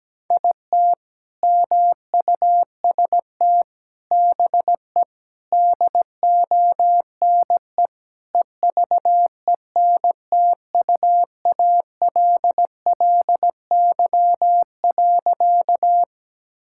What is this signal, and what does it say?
Morse code, heard as a keyed tone, about 17 wpm, 700 Hz. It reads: IT MUST BE DONE EVENTUALLY.
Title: Sentences from Top 1000 Words - Review Part 6 - 17wpm